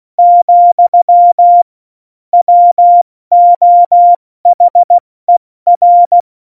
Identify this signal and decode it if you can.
Morse code, heard as a keyed tone, about 16 wpm, spaced as for 14 wpm, 700 Hz. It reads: , WOHER
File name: Morsekurs / Witze / Witz 04. September 2018, 70 F80